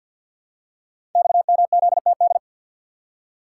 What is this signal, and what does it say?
Morse code, heard as a keyed tone, about 47 wpm, 700 Hz. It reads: XMZTD
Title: Code Groups - 5 Letter:Numbers - 50wpm